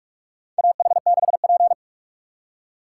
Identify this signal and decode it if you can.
Morse code, heard as a keyed tone, about 46 wpm, 690 Hz. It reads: AH6P
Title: USA Call Signs - 4 Characters - 2x Character Spacing - 45wpm